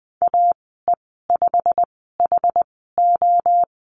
Morse code, heard as a keyed tone, about 20 wpm, spaced as for 15 wpm, 700 Hz. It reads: AE5HO
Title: 193 Lesson 31g - Letter v - Call Signs 20wpm